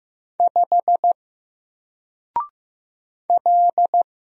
Morse code, heard as a keyed tone, about 15 wpm, 700 Hz. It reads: 5 L
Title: Single Letter-Number 15wpm